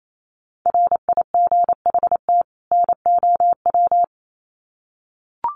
Morse code, heard as a keyed tone, about 28 wpm, 700 Hz. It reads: RIGHT NOW
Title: Sets of 2 Words 28wpm